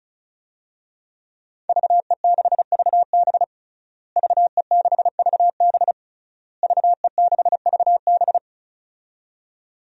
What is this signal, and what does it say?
Morse code, heard as a keyed tone, about 36 wpm, 700 Hz. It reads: VE6VB VE6VB VE6VB
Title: DXCC-100 Call Signs - 5 Characters - Repeated 3x - 35wpm